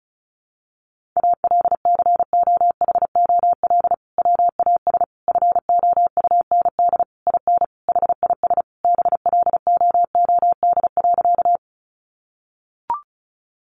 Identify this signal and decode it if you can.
Morse code, heard as a keyed tone, about 34 wpm, 700 Hz. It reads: ALCOHOL WAS FOUND IN HIS BLOOD.
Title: Sentences from Top 1400 Words 35wpm